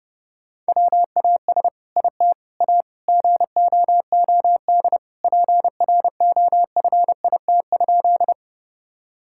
Morse code, heard as a keyed tone, about 32 wpm, 700 Hz. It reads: WAS IT A GOOD PROFIT?
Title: Sentences from Top 1000 Words - Review Part 5 - 30wpm